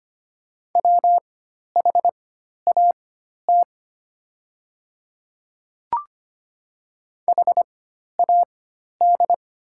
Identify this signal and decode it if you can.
Morse code, heard as a keyed tone, about 25 wpm, spaced as for 12 wpm, 700 Hz. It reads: WHAT HAD
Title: 4x Character Spacing - Top 100 Words - 25wpm